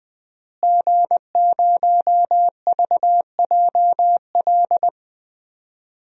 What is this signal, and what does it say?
Morse code, heard as a keyed tone, about 20 wpm, 700 Hz. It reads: G0VJL
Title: DXCC-100 Call Signs - 5 Characters - Repeated 3x - 20wpm